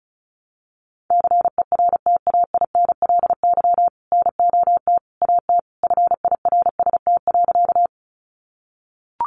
Morse code, heard as a keyed tone, about 34 wpm, 700 Hz. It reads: CERTAINLY NOT AT FIRST.
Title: Sentences from Top 900 Words 35wpm